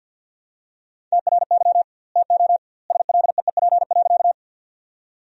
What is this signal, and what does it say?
Morse code, heard as a keyed tone, about 52 wpm, 700 Hz. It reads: TRY TO SLEEP.
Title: 2x Word-Spacing - Sentences from Top 500 Words - 50wpm